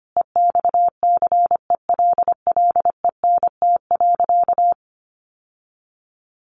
Morse code, heard as a keyed tone, about 25 wpm, 700 Hz. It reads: EXCELLENT.
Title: Sentences from Top 1000 Words 25wpm